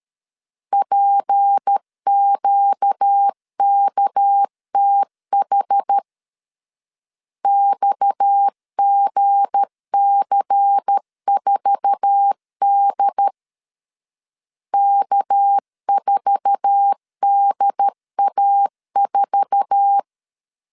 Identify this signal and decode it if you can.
Morse code, heard as a keyed tone, about 13 wpm, 800 Hz. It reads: PQKTH XGC4D K4DA4
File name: LEZIONI / 8 YJ45 / 8 YJ4 12wpm 2151